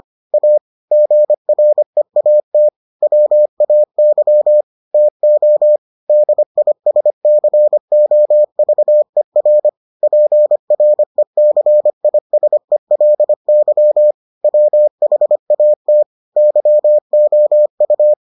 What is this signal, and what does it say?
Morse code, heard as a keyed tone, about 25 wpm, 600 Hz. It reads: A GREAT WAY TO DISCOVER PRECISELY WHAT YOU